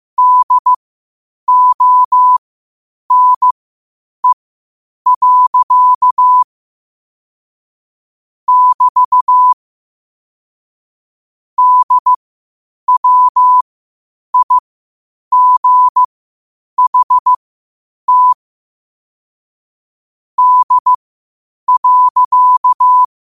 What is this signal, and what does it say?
Morse code, heard as a keyed tone, about 15 wpm, spaced as for 8 wpm, 1000 Hz. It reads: DONE. <BT> DWIGHT D.
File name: qotdfiles / 8wpm